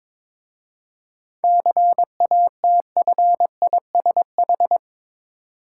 Morse code, heard as a keyed tone, about 22 wpm, 690 Hz. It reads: CATFISH
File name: Difficult Words with Hints - 22wpm